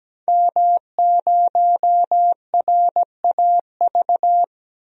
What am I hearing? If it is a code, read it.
Morse code, heard as a keyed tone, about 17 wpm, 700 Hz. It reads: M0RAV
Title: DXCC-100 Call Signs - 5 Characters - 17wpm